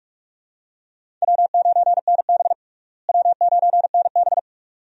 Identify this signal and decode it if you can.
Morse code, heard as a keyed tone, about 46 wpm, 700 Hz. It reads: W9NB W9NB
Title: USA Call Signs - 4 Characters - Repeated 2x - 45wpm